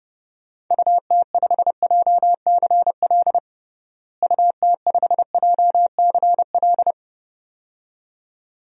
Morse code, heard as a keyed tone, about 30 wpm, 700 Hz. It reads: UT5JCL UT5JCL
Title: DXCC-100 Call Signs - 6 Characters - Repeated 2x - 30wpm